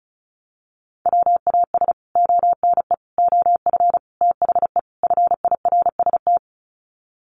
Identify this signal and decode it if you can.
Morse code, heard as a keyed tone, about 34 wpm, 700 Hz. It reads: WAS ONE OF THE FIRST
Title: Sets of 5 Words 35wpm